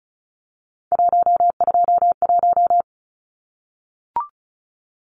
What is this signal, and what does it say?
Morse code, heard as a keyed tone, about 34 wpm, 700 Hz. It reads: 121
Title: Numbers - 3-Digit 35wpm